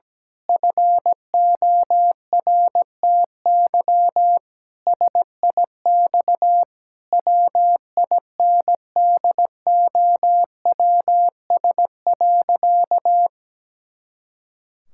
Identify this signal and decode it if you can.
Morse code, heard as a keyed tone, about 17 wpm, 700 Hz. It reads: FORTY SIX WINDOWS.
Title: Pangram Challenge v2 - 17wpm